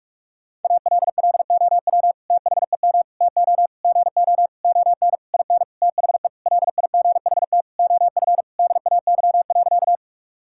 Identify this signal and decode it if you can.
Morse code, heard as a keyed tone, about 46 wpm, 700 Hz. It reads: ALLOW THEM TO GO ON IN THE LIGHT OF DAY.
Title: Sentences from Top 1000 Words - Review Part 5 - 45wpm